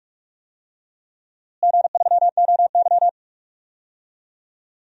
Morse code, heard as a keyed tone, about 46 wpm, 700 Hz. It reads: G3OY
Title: DXCC-100 Call Signs - 4 Characters - Repeated 3x - 45wpm